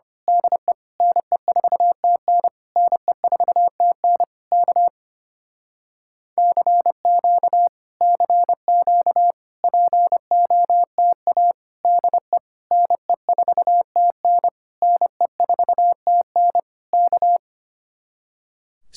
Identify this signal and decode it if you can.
Morse code, heard as a keyed tone, about 25 wpm, 700 Hz. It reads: DE NE4TN NE4TN K CQ CQ POTA DE NE4TN NE4TN K
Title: Speed-Racing Common QSO Phrases - 25wpm